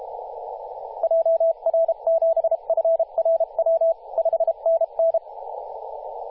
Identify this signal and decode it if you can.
Morse code, heard as a keyed tone, about 32 wpm, 660 Hz, with radio noise. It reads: JR7FRW 5NN
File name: posts / 2016-oct-05-2124z-t31t-18-072mhz-cw